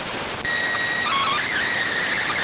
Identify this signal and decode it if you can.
SSTV (slow-scan television), PD 290.